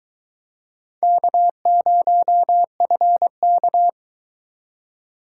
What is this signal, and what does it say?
Morse code, heard as a keyed tone, about 24 wpm, 700 Hz. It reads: K0FK